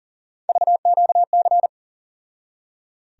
Morse code, heard as a keyed tone, about 39 wpm, 700 Hz. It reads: VQC
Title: USA Call Sign Postfixes - 3 Letters 40wpm